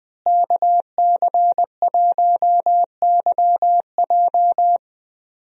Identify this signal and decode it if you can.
Morse code, heard as a keyed tone, about 20 wpm, 700 Hz. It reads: KC1YJ
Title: USA Call Signs - 5 Characters - Repeated 3x - 20wpm